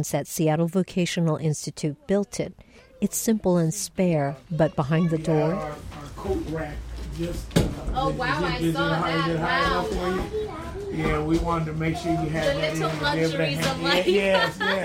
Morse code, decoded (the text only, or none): none